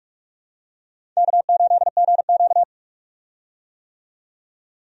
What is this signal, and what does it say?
Morse code, heard as a keyed tone, about 43 wpm, 700 Hz. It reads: K8GQ
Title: USA Call Signs - 4 Characters - 45wpm